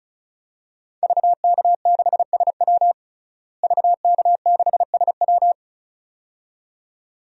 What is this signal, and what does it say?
Morse code, heard as a keyed tone, about 34 wpm, 700 Hz. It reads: VK6SW VK6SW
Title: DXCC-100 Call Signs - 5 Characters - Repeated 2x - 35wpm